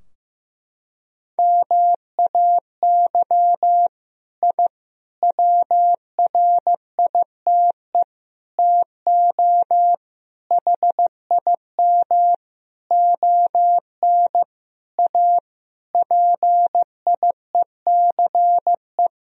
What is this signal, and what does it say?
Morse code, heard as a keyed tone, about 15 wpm, 700 Hz. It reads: MAY I WRITE TO HIM ON A PIECE
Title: Sentences from Top 1000 Words - Review Part 2 - 15wpm